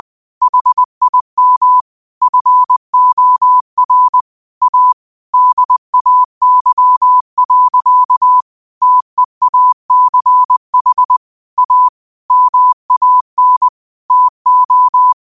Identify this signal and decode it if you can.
Morse code, heard as a keyed tone, about 20 wpm, 1000 Hz. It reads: HIM FOR A DAY. TEACH A MAN TO
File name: qotdfiles / 20wpm